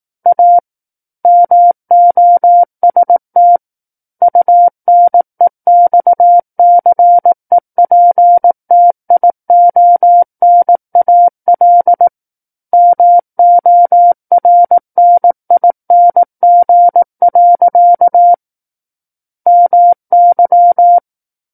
Morse code, heard as a keyed tone, about 18 wpm, 700 Hz. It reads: A MOST UNEXCEPTIONAL MORNING. MY
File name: 18 WPM